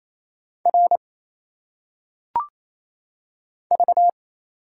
Morse code, heard as a keyed tone, about 27 wpm, 700 Hz. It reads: R V
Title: Single Letters - Phonetic Pronunciation 28wpm